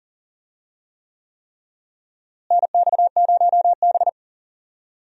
Morse code, heard as a keyed tone, about 40 wpm, 700 Hz. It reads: NX0B
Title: USA Call Signs - 4 Characters - Repeated 3x - 40wpm